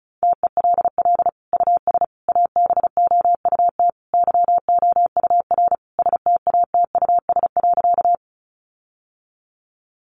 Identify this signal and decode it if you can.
Morse code, heard as a keyed tone, about 34 wpm, 700 Hz. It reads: TELL US ABOUT YOUR STATUS.
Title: Sentences from Top 1400 Words 35wpm